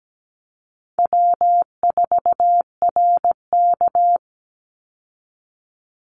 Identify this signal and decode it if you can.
Morse code, heard as a keyed tone, about 17 wpm, 700 Hz. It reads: W4RK
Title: Easy Call Signs Repeated 2x - 17wpm